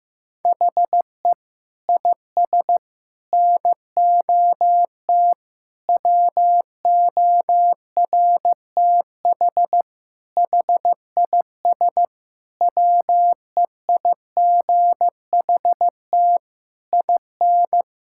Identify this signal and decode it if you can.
Morse code, heard as a keyed tone, about 15 wpm, 700 Hz. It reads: HE IS NOT WORTH HIS WEIGHT IN
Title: Sentences from Top 1000 Words - Review Part 5 - 15wpm